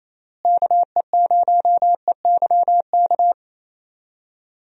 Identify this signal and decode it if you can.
Morse code, heard as a keyed tone, about 28 wpm, 700 Hz. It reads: KE0EYK
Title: USA Call Signs - 6 Characters - Repeated 3x - 28wpm